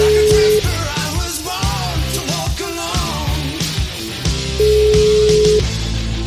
Ringback tone (Europe, 425 Hz), with noise.